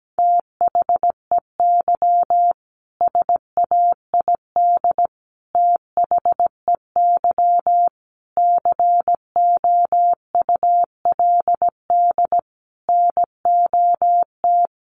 Morse code, heard as a keyed tone, about 17 wpm, 700 Hz. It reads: THEY SAID THEY COULD NOT
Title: Sentences from Top 1400 Words 17wpm